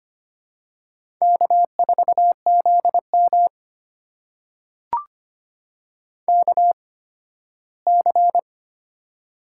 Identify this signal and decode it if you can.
Morse code, heard as a keyed tone, about 25 wpm, spaced as for 9 wpm, 700 Hz. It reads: K4ZMK C
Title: USA Call Signs - 8x Character Spacing - 25wpm